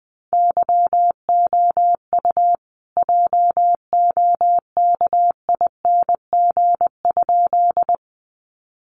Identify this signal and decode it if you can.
Morse code, heard as a keyed tone, about 20 wpm, 700 Hz. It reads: YOU JOKING?
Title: A Sound Of Thunder 20wpm